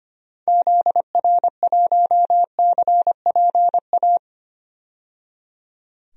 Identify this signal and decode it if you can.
Morse code, heard as a keyed tone, about 25 wpm, 700 Hz. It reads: ZR1CPA
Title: DXCC-100 Call Signs - 6 Characters - 25wpm